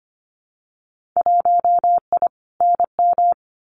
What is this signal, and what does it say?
Morse code, heard as a keyed tone, about 25 wpm, 700 Hz. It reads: 1I NM
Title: Winter Field Day Exchanges Encore 25wpm